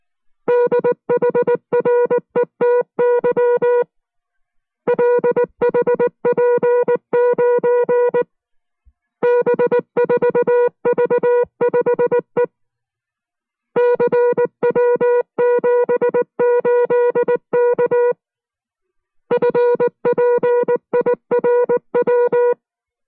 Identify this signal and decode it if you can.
Morse code, heard as a keyed tone, about 19 wpm, 480 Hz. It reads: DHRET YLHP9 B4V5E CW78K FPIRW